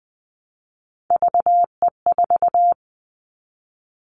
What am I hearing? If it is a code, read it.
Morse code, heard as a keyed tone, about 20 wpm, 700 Hz. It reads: VE4
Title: Callsign Prefixes for Top 100 DXCC - 20wpm